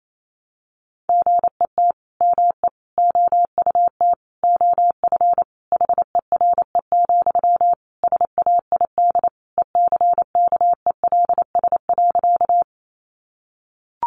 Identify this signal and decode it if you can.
Morse code, heard as a keyed tone, about 28 wpm, 700 Hz. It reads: GET ME OUT OF HERE, SAID ECKELS.
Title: A Sound Of Thunder 28wpm